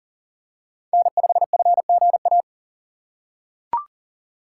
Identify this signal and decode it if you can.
Morse code, heard as a keyed tone, about 39 wpm, 700 Hz. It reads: N5FGA
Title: USA Call Signs - 5 Characters - 4x Character Spacing - 40wpm